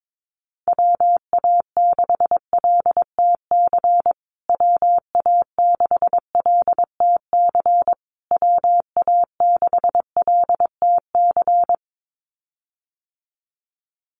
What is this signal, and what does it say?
Morse code, heard as a keyed tone, about 22 wpm, 700 Hz. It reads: WA6LTC WA6LTC WA6LTC
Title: USA Call Signs Repeated 3x 22wpm